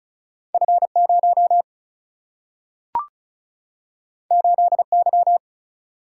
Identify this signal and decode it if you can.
Morse code, heard as a keyed tone, about 36 wpm, 700 Hz. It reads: F0 8Y
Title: Single-Letter Single-Number 35wpm